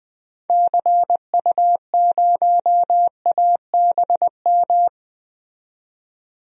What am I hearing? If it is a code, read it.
Morse code, heard as a keyed tone, about 20 wpm, 700 Hz. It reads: CU0ABM
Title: DXCC-100 Call Signs - 6 Characters - 20wpm